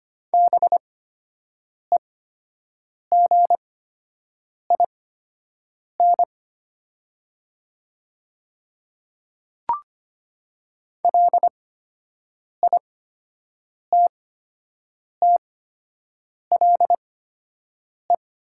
Morse code, heard as a keyed tone, about 25 wpm, spaced as for 7 wpm, 700 Hz. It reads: BEGIN LITTLE
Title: 8x Character Spacing - Top 300 Words - 25wpm